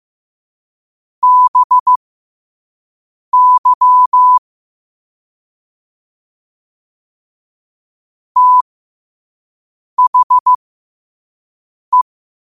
Morse code, heard as a keyed tone, about 15 wpm, spaced as for 5 wpm, 1000 Hz. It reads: BY THE
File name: qotdfiles / 5wpm